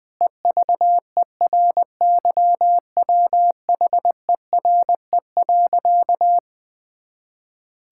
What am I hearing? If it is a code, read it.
Morse code, heard as a keyed tone, about 21 wpm, 700 Hz. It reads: EVERYWHERE.
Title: Sentences from Top 1600 Words 20wpm